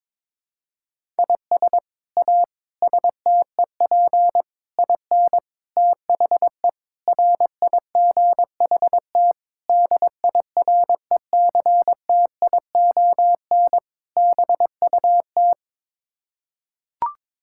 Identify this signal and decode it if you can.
Morse code, heard as a keyed tone, about 22 wpm, 700 Hz. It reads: IS A STEP IN THE RIGHT DIRECTION BUT
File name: practice / 22wpm / Sets of 8 Words - 22wpm